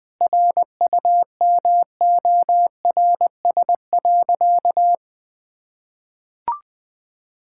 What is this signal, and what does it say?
Morse code, heard as a keyed tone, about 20 wpm, 700 Hz. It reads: RUMORS.
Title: Sentences with 6-Word Phrases - 20wpm